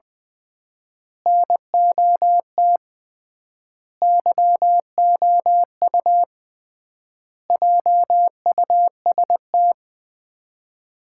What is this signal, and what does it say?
Morse code, heard as a keyed tone, about 20 wpm, 700 Hz. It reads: NOT YOU JUST
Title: 3x Word-Spacing - Sets of 5 Words - 20wpm